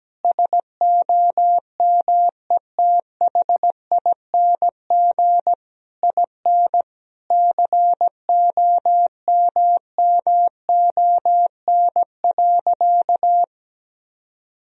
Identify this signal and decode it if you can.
Morse code, heard as a keyed tone, about 17 wpm, 700 Hz. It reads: SOMETHING IN COMMON.
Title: Sentences from Top 1000 Words - Review Part 3 - 17wpm